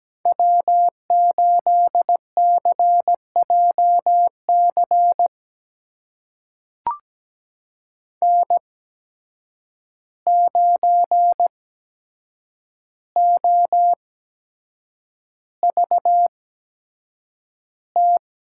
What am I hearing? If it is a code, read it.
Morse code, heard as a keyed tone, about 17 wpm, spaced as for 7 wpm, 700 Hz. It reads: W8CJC N9OVT
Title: USA Call Signs - 5 Characters - 8x Character Spacing - 17wpm